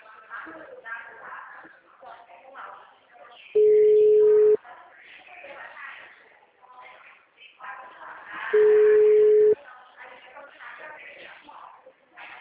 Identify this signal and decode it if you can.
Ringback tone (Europe, 425 Hz), with speech.
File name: out- -2003- - - .30995